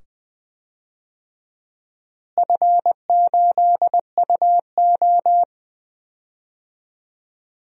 Morse code, heard as a keyed tone, about 20 wpm, 700 Hz. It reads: F8UO
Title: DXCC-100 Call Signs - 4 Characters - Repeated 2x - 20wpm